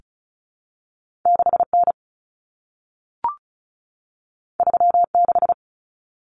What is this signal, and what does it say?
Morse code, heard as a keyed tone, about 36 wpm, 700 Hz. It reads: 6N 36
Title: Common Cut Numbers - 2-Digit 35wpm